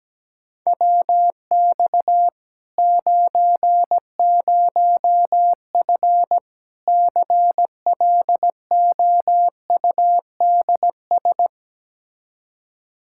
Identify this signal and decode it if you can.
Morse code, heard as a keyed tone, about 17 wpm, 700 Hz. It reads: WX 90F CLOUDS
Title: Practice QSO - Exchange 2 - Info - 17wpm